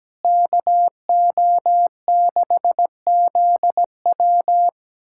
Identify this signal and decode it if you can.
Morse code, heard as a keyed tone, about 17 wpm, 700 Hz. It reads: KO6ZW
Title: USA Callsign DE Callsign - 5 Characters - 17wpm